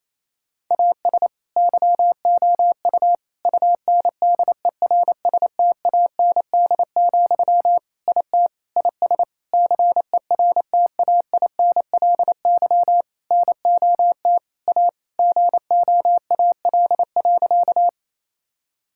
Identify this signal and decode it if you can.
Morse code, heard as a keyed tone, about 28 wpm, 700 Hz. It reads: AS YOU UNDERSTAND, IT IS CERTAINLY NOT A GOAL.
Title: Sentences from Top 1000 Words - Review Part 5 - 28wpm